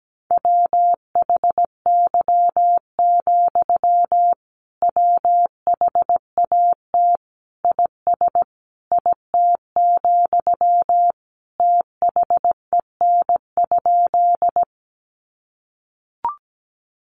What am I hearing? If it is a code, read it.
Morse code, heard as a keyed tone, about 17 wpm, 700 Hz. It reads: WHY, WHAT IS IT, THEN?
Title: Sentences from Top 200 Words 17wpm